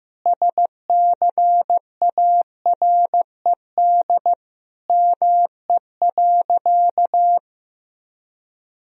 Morse code, heard as a keyed tone, about 15 wpm, 700 Hz. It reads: SCARED ME.
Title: Sentences from Top 1000 Words - Review Part 4 - 15wpm